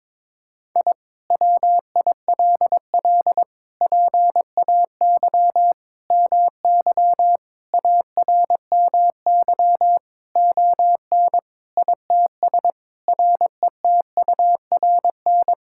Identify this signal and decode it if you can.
Morse code, heard as a keyed tone, about 22 wpm, 700 Hz. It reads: I WILL PAY MY ARMY ON ITS RETURN
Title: Sentences from Top 1000 Words - Review Part 5 - 22wpm